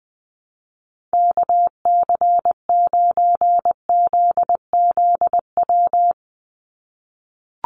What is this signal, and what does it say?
Morse code, heard as a keyed tone, about 20 wpm, 700 Hz. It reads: KC9ZZW
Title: USA Call Signs 20wpm